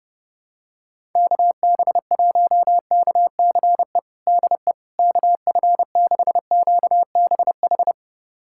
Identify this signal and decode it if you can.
Morse code, heard as a keyed tone, about 30 wpm, 700 Hz. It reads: KB1KCE DE KF6QBH
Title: USA Callsign DE Callsign 30wpm